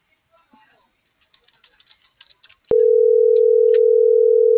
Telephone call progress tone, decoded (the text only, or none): ringback tone